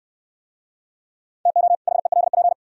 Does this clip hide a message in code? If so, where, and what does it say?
Yes: Morse code from 1.5 to 2.5 s, heard as a keyed tone, about 68 wpm, 700 Hz. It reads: TO HELP